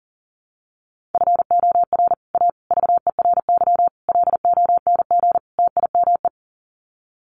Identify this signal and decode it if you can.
Morse code, heard as a keyed tone, about 40 wpm, 700 Hz. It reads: FOR A VERY LONG TIME
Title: Sets of 5 Words 40wpm